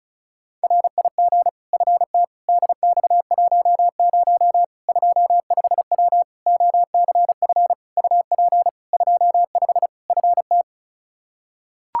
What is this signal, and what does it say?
Morse code, heard as a keyed tone, about 36 wpm, 700 Hz. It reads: RIG FT DX10 25W OCF UP 25 FT
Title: Practice QSO - Exchange 2 - Info - 35wpm